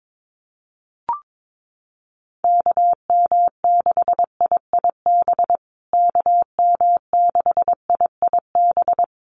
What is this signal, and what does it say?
Morse code, heard as a keyed tone, about 22 wpm, 700 Hz. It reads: KM6IIB KM6IIB
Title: USA Call Signs Repeated 2x 22wpm